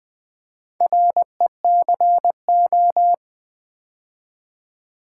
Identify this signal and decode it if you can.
Morse code, heard as a keyed tone, about 20 wpm, 690 Hz. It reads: RECO
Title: Build A Word - 2 Letters At A Time - 6-Letter Words - 20wpm